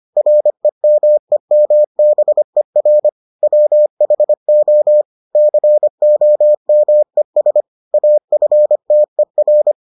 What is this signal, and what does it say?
Morse code, heard as a keyed tone, about 25 wpm, 600 Hz. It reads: REMEMBER WHO COMES AFTER